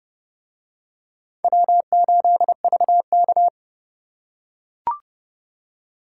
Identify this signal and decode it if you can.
Morse code, heard as a keyed tone, about 30 wpm, 700 Hz. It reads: W8VK